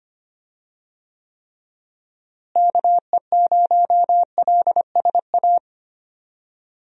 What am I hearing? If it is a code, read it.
Morse code, heard as a keyed tone, about 25 wpm, 700 Hz. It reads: KE0LSA
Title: USA Call Signs - 6 Characters - Repeated 2x - 25wpm